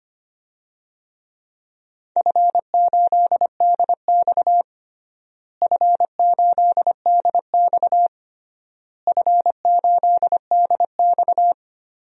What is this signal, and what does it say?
Morse code, heard as a keyed tone, about 25 wpm, 700 Hz. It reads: F8DX F8DX F8DX
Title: DXCC-100 Call Signs - 4 Characters - Repeated 3x - 25wpm